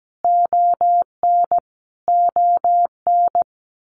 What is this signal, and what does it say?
Morse code, heard as a keyed tone, about 17 wpm, 700 Hz. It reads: ON ON
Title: Canadian Provinces and Territories - Repeated 3x - 17wpm